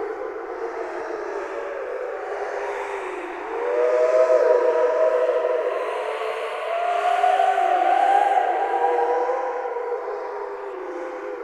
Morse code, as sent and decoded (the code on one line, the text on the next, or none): none